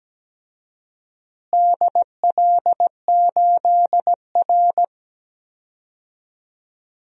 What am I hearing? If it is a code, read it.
Morse code, heard as a keyed tone, about 17 wpm, 700 Hz. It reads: DL8R
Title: DXCC-100 Call Signs - 4 Characters - Repeated 2x - 17wpm